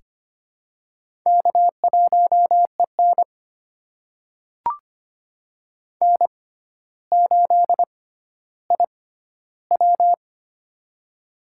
Morse code, heard as a keyed tone, about 25 wpm, spaced as for 13 wpm, 700 Hz. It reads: K1EN N8IW
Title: USA Call Signs - 4 Characters - 6x Character Spacing - 25wpm